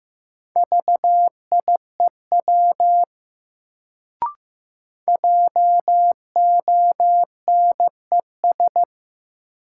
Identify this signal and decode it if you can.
Morse code, heard as a keyed tone, about 15 wpm, 700 Hz. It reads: VIEW JONES
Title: Sets of 5 Words - New Word Drill - 15wpm